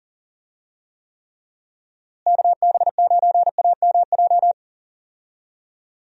Morse code, heard as a keyed tone, about 38 wpm, 700 Hz. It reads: KB9AMJ
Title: USA Call Signs - 6 Characters - Repeated 3x - 40wpm